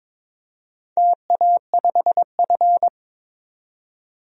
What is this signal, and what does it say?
Morse code, heard as a keyed tone, about 22 wpm, 700 Hz. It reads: TA5F
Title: DXCC-100 Call Signs - 4 Characters - 22wpm